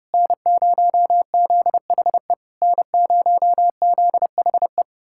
Morse code, heard as a keyed tone, about 33 wpm, 700 Hz. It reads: N0ZHE N0ZHE